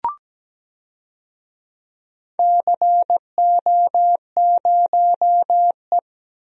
Morse code, heard as a keyed tone, about 17 wpm, 700 Hz. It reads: CO0E
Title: DXCC-100 Call Signs - 4 Characters - Repeated 2x - 17wpm